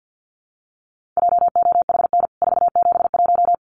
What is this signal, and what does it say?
Morse code, heard as a keyed tone, about 49 wpm, 700 Hz. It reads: JOHN 471
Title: CWT Exchange 50wpm